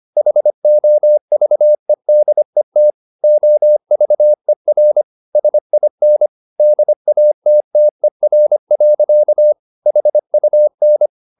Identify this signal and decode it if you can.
Morse code, heard as a keyed tone, about 25 wpm, 600 Hz. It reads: HOVEDET OVER SIN DATTER. HUN
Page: Ferie 25 wpm